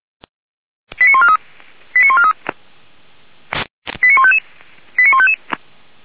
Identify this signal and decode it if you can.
Five-tone selective call (ZVEI), heard as quick runs of five tones, tones of about 70 ms. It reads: 89143 89143 89155 89155
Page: ZVEI